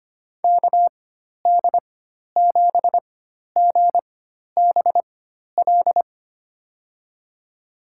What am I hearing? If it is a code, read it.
Morse code, heard as a keyed tone, about 25 wpm, spaced as for 12 wpm, 700 Hz. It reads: KD7GBL
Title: USA Call Signs - 6 Characters - 4x Character Spacing - 25wpm